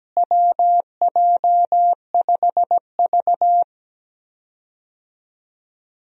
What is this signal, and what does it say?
Morse code, heard as a keyed tone, about 17 wpm, 700 Hz. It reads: WJ5V
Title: USA Call Signs - 4 Characters - Repeated 3x - 17wpm